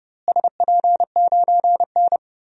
Morse code, heard as a keyed tone, about 31 wpm, 700 Hz. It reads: SP9N
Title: DXCC-100 Call Signs - 4 Characters - Repeated 2x - 30wpm